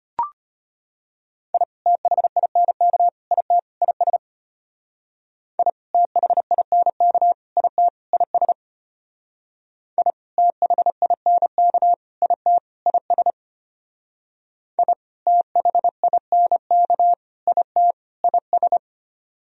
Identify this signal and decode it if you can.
Morse code, heard as a keyed tone, about 33 wpm, 700 Hz. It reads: I THINK IT IS I THINK IT IS I THINK IT IS I THINK IT IS